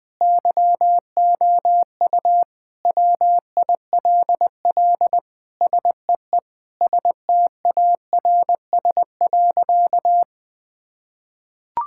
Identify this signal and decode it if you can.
Morse code, heard as a keyed tone, about 20 wpm, 700 Hz. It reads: YOU WILL SEE STARS.
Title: Sentences with 4-Word Phrases - 20wpm